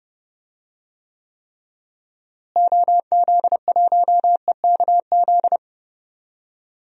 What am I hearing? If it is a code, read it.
Morse code, heard as a keyed tone, about 30 wpm, 700 Hz. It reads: OZ1EKZ
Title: DXCC-100 Call Signs - 6 Characters - Repeated 3x - 30wpm